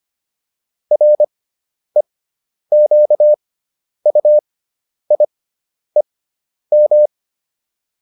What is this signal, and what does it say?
Morse code, heard as a keyed tone, about 25 wpm, spaced as for 10 wpm, 600 Hz. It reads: REQUIEM